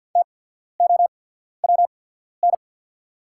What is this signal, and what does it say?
Morse code, heard as a keyed tone, about 48 wpm, spaced as for 14 wpm, 690 Hz. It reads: TOWN
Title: 8x Character Spacing - Top 500 Words - 50wpm